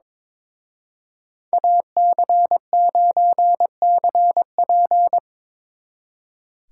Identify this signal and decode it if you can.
Morse code, heard as a keyed tone, about 25 wpm, 700 Hz. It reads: AC9CP
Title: Speed Racing - USA Call Signs 22wpm